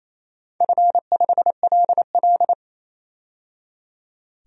Morse code, heard as a keyed tone, about 29 wpm, 700 Hz. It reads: F5LL
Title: DXCC-100 Call Signs - 4 Characters - 28wpm